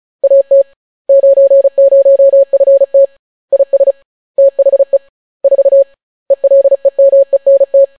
Morse code, heard as a keyed tone, about 36 wpm, 550 Hz. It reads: AT 90FT IS THE 4 ELEMENT